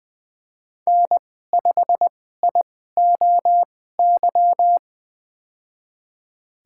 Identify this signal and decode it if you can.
Morse code, heard as a keyed tone, about 20 wpm, spaced as for 14 wpm, 700 Hz. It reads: N5IOY
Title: USA Call Signs - 5 Characters - 2x Character Spacing - 20wpm